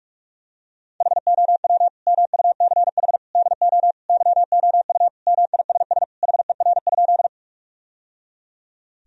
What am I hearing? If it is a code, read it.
Morse code, heard as a keyed tone, about 46 wpm, 700 Hz. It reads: HOW MUCH DO YOU MISS HER?
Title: Sentences from Top 1000 Words - Review Part 3 - 45wpm